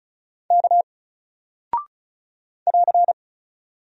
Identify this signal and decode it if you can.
Morse code, heard as a keyed tone, about 36 wpm, 700 Hz. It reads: K <AR>
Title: Single Character (Letter:Number:Punctuation:Prosign) Alt 35wpm